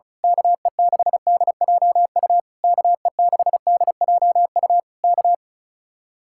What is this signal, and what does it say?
Morse code, heard as a keyed tone, about 32 wpm, 700 Hz. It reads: KE6DJU KE6DJU K
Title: Speed-Racing Common QSO Phrases - 35wpm